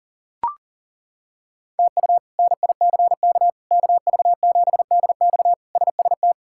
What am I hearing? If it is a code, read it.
Morse code, heard as a keyed tone, about 37 wpm, 700 Hz. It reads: TU NICK KV7DX SST
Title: K1USN SST QSO Practice - 40wpm